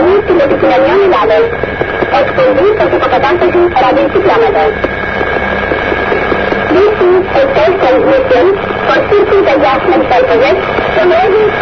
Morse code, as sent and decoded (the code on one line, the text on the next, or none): none